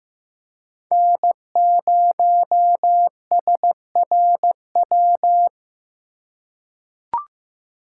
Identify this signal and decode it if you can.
Morse code, heard as a keyed tone, about 15 wpm, 700 Hz. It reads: N0SRW
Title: USA Call Signs - 5 Characters - 2x Character Spacing - 15wpm